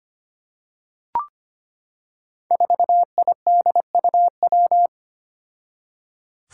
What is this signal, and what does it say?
Morse code, heard as a keyed tone, about 25 wpm, 700 Hz. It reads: 4IDUW
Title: Five Challenging Letter-Number Combinations 25wpm